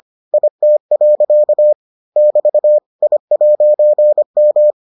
Morse code, heard as a keyed tone, about 25 wpm, 600 Hz. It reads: IT. <BT> I'M